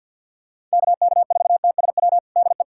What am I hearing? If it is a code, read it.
Morse code, heard as a keyed tone, about 49 wpm, 700 Hz. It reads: KK4TSW DE
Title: USA Callsign DE Callsign - 6 Characters - 50wpm